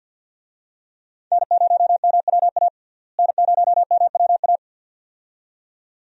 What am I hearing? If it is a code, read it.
Morse code, heard as a keyed tone, about 50 wpm, 700 Hz. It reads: N0MWA N0MWA
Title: USA Call Signs - 5 Characters - Repeated 2x - 50wpm